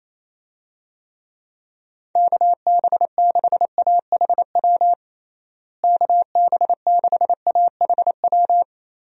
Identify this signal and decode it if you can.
Morse code, heard as a keyed tone, about 27 wpm, 700 Hz. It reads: KB6AHW KB6AHW
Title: USA Call Signs - 6 Characters - Repeated 3x - 28wpm